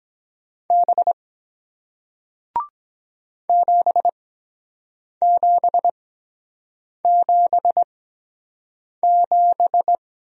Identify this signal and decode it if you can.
Morse code, heard as a keyed tone, about 23 wpm, spaced as for 7 wpm, 700 Hz. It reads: B 7777